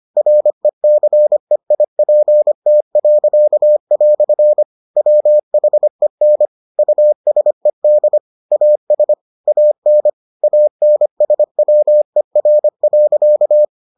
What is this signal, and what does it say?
Morse code, heard as a keyed tone, about 25 wpm, 600 Hz. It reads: RECEIPT." WHEN USED AS AN ANSWER.